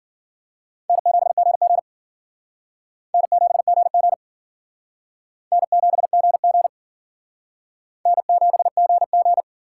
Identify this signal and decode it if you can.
Morse code, heard as a keyed tone, about 52 wpm, 700 Hz. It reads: N7GG N7GG N7GG N7GG